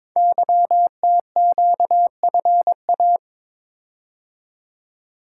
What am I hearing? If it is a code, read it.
Morse code, heard as a keyed tone, about 22 wpm, 700 Hz. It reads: YTQFA
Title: Code Groups - 5 Letters - 22wpm